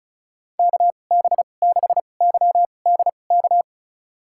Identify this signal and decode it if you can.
Morse code, heard as a keyed tone, about 36 wpm, spaced as for 25 wpm, 700 Hz. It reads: KB6YDK
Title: USA Call Signs - 6 Characters - 2x Character Spacing - 35wpm